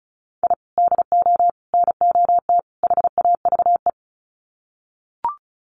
Morse code, heard as a keyed tone, about 34 wpm, 700 Hz. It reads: I DO NOT HAVE
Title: Sets of 4 Words 35wpm